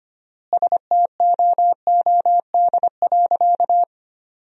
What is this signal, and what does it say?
Morse code, heard as a keyed tone, about 25 wpm, 700 Hz. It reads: STOOD.
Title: 4x Word-Spacing - Sentences from Top 500 Words - 25wpm